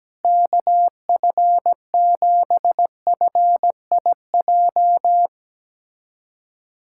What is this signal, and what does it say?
Morse code, heard as a keyed tone, about 17 wpm, 700 Hz. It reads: KF7FIJ
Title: USA Callsign DE Callsign - 6 Characters - 17wpm